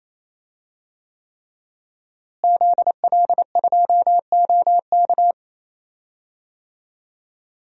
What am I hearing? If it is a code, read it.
Morse code, heard as a keyed tone, about 28 wpm, 700 Hz. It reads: ZL2OK
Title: DXCC-100 Call Signs - 5 Characters - Repeated 3x - 28wpm